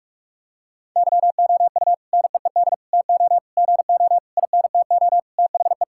Morse code, heard as a keyed tone, about 46 wpm, 700 Hz. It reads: YOU NEED TO GO INTO THE
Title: Sentences from Top 1000 Words - Review Part 6 - 45wpm